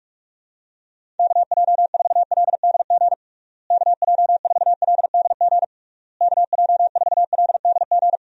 Morse code, heard as a keyed tone, about 46 wpm, 700 Hz. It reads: KJ4LDG KJ4LDG KJ4LDG
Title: USA Call Signs - 6 Characters - Repeated 3x - 45wpm